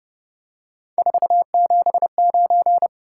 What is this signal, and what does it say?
Morse code, heard as a keyed tone, about 30 wpm, 700 Hz. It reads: 479
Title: Top 180 QSO Elements 30wpm